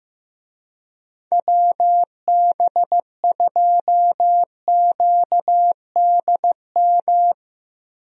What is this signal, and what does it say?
Morse code, heard as a keyed tone, about 15 wpm, 700 Hz. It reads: WB2QDM
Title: USA Call Signs - 6 Characters - Repeated 3x - 15wpm